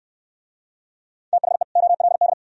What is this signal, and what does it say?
Morse code, heard as a keyed tone, about 69 wpm, 700 Hz. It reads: THE OLD